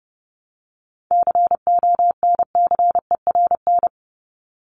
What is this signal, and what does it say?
Morse code, heard as a keyed tone, about 30 wpm, 700 Hz. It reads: CONCERN
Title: Top 751-1000 Words 30wpm